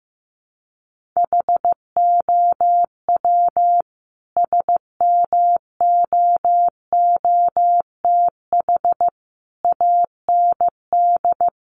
Morse code, heard as a keyed tone, about 15 wpm, 700 Hz. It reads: HOW SMOOTH AND
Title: Sentences from Top 1400 Words 15wpm